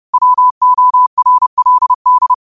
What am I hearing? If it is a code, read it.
Morse code, heard as a keyed tone, about 30 wpm, 1000 Hz. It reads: WORLD
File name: qotdfiles / 30wpm